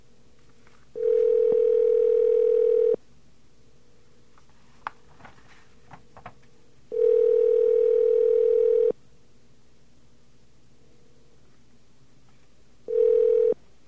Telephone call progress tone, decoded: ringback tone